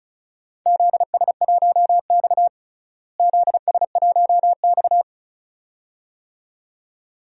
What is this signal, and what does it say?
Morse code, heard as a keyed tone, about 36 wpm, 700 Hz. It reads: ZS1X ZS1X
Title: DXCC-100 Call Signs - 4 Characters - Repeated 2x - 35wpm